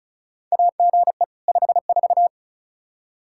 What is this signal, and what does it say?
Morse code, heard as a keyed tone, about 34 wpm, 700 Hz. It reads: AGE 54
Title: Practice QSO - Exchange 3 - More Info - 35wpm